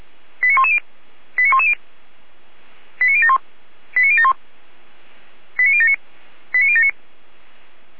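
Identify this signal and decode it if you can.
Five-tone selective call (ZVEI), heard as quick runs of five tones, tones of about 70 ms. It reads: 89110 89110 89071 89071 89079 89079
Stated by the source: Five-tone ZVEI selective call received by SDR, from the Feuerwehr Orlamünde fire brigade audio server.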